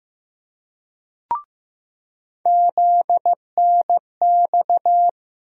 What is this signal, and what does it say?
Morse code, heard as a keyed tone, about 15 wpm, 700 Hz. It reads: ZNX
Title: USA Call Sign Postfixes - 3 Letters 15wpm